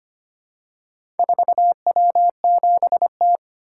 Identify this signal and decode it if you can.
Morse code, heard as a keyed tone, about 25 wpm, 700 Hz. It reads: 4W7T
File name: practice / 25wpm / Four Letter-Number Combinations - 25wpm